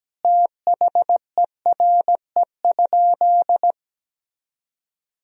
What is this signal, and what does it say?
Morse code, heard as a keyed tone, about 17 wpm, 700 Hz. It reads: THERE?
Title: Sentences from Top 1000 Words - Review Part 3 - 17wpm